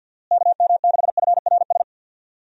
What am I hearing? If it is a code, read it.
Morse code, heard as a keyed tone, about 45 wpm, 700 Hz. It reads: KM6FRS
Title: USA Call Signs - 6 Characters - Repeated 3x - 50wpm